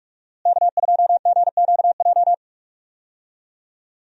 Morse code, heard as a keyed tone, about 43 wpm, 700 Hz. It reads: K2GQJ
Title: USA Call Signs - 5 Characters - Repeated 2x - 45wpm